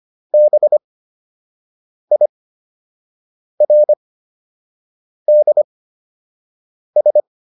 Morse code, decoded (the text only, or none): BIRDS